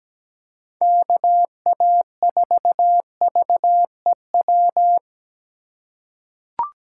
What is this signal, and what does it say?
Morse code, heard as a keyed tone, about 17 wpm, 700 Hz. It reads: KA4VEW
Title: USA Call Signs - 6 Characters - 4x Character Spacing - 17wpm